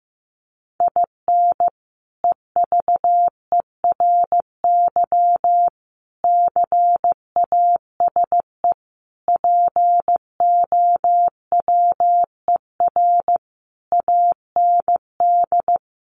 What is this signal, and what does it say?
Morse code, heard as a keyed tone, about 15 wpm, 700 Hz. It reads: IN EVERY CASE POWER AND
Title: Sentences from Top 1200 Words 15wpm